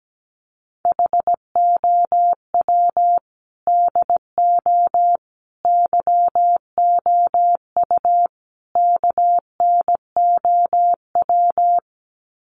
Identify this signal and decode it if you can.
Morse code, heard as a keyed tone, about 17 wpm, 700 Hz. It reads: HOW DO YOU KNOW
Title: Sets of 4 Words 17wpm